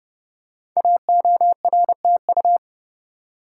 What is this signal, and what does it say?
Morse code, heard as a keyed tone, about 30 wpm, 700 Hz. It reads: AORTU
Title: Code Groups - 5 Letters - 30wpm